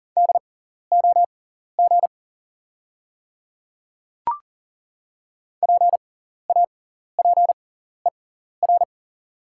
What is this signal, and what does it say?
Morse code, heard as a keyed tone, about 40 wpm, spaced as for 14 wpm, 700 Hz. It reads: DOG PAPER
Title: 6x Character Spacing - Top 500 Words - 40wpm